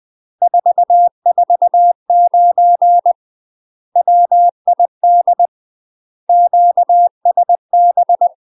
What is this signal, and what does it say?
Morse code, heard as a keyed tone, about 20 wpm, 700 Hz. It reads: 449 WID QSB
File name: UK MASTER QSOs TRAINING / 20WPM / UK MASTER QSOs 0011